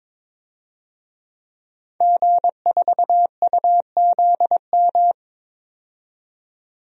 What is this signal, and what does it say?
Morse code, heard as a keyed tone, about 22 wpm, 700 Hz. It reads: G4UZM
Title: DXCC-100 Call Signs - 5 Characters - 22wpm